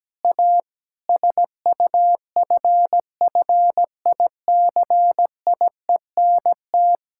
Morse code, heard as a keyed tone, about 17 wpm, 700 Hz. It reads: A SUFFICIENT
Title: Sentences from Top 1600 Words 17wpm